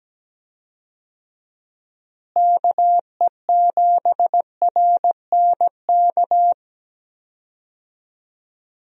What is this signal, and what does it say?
Morse code, heard as a keyed tone, about 17 wpm, 700 Hz. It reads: KE7RNK
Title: USA Call Signs - 6 Characters - Repeated 2x - 17wpm